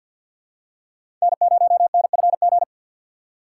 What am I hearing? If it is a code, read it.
Morse code, heard as a keyed tone, about 48 wpm, 700 Hz. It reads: N0NFG
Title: USA Call Signs - 5 Characters - 3x Character Spacing - 50wpm